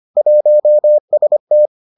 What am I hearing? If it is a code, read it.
Morse code, heard as a keyed tone, about 25 wpm, 600 Hz. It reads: 1ST